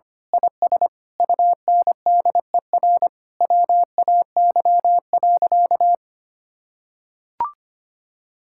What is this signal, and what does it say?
Morse code, heard as a keyed tone, about 25 wpm, 700 Hz. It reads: IS UNDER WAY.
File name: practice / 25wpm / Sentences with 9-Word Phrases - 25wpm